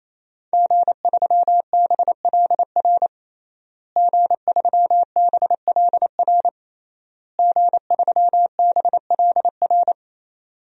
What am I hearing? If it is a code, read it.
Morse code, heard as a keyed tone, about 27 wpm, 700 Hz. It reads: G3BLR G3BLR G3BLR
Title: DXCC-100 Call Signs - 5 Characters - Repeated 3x - 28wpm